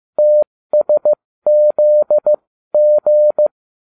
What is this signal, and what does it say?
Morse code, heard as a keyed tone, about 15 wpm, spaced as for 13 wpm, 600 Hz. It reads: TSZG